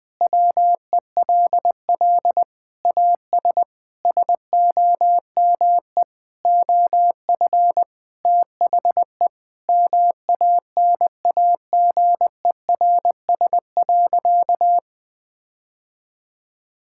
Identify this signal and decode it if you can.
Morse code, heard as a keyed tone, about 20 wpm, 700 Hz. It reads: WELL AS SOME OF THE MANAGERS.
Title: Sentences with 6-Word Phrases - 20wpm